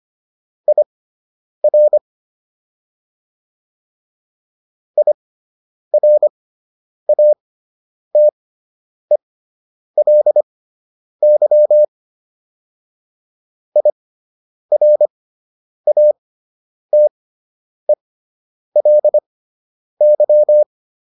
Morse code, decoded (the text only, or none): IR IRATELY IRATELY